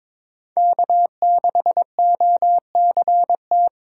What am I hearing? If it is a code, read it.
Morse code, heard as a keyed tone, about 26 wpm, 700 Hz. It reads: K6OCT
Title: Speed Racing - USA 1x3 Call Signs 22wpm